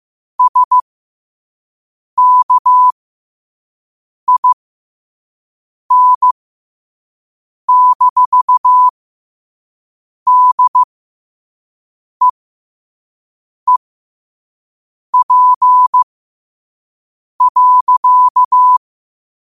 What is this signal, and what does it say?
Morse code, heard as a keyed tone, about 15 wpm, spaced as for 5 wpm, 1000 Hz. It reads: SKIN-DEEP.
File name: qotdfiles / 5wpm